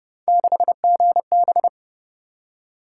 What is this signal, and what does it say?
Morse code, heard as a keyed tone, about 27 wpm, 700 Hz. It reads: 6GB